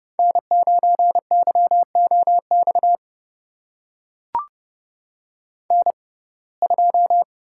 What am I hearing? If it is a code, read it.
Morse code, heard as a keyed tone, about 30 wpm, spaced as for 22 wpm, 700 Hz. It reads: N9YOX N2
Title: USA Call Signs - 5 Characters - 6x Character Spacing - 30wpm